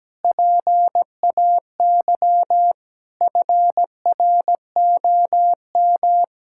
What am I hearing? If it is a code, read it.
Morse code, heard as a keyed tone, about 17 wpm, 700 Hz. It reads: PAY FROM
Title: Pangram Challenge v2 - 17wpm